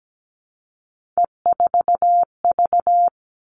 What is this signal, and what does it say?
Morse code, heard as a keyed tone, about 17 wpm, 700 Hz. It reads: E4V
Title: All Cut Numbers - 3-Digit 17wpm